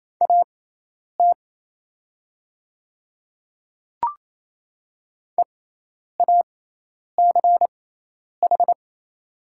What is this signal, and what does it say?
Morse code, heard as a keyed tone, about 27 wpm, spaced as for 10 wpm, 700 Hz. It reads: AT EACH
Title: 6x Character Spacing - Top 200 Words - 28wpm